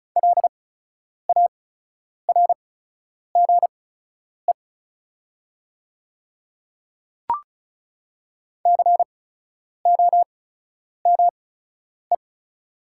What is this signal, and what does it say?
Morse code, heard as a keyed tone, about 34 wpm, spaced as for 10 wpm, 700 Hz. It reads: LARGE COME
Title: 8x Character Spacing - Top 200 Words - 35wpm